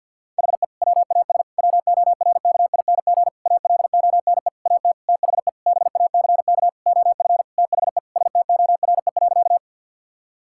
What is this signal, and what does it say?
Morse code, heard as a keyed tone, about 50 wpm, 700 Hz. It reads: HE WAS WORKING ALONE AT THE BACK OF THE STORE.
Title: Sentences from Top 1000 Words - Review Part 5 - 50wpm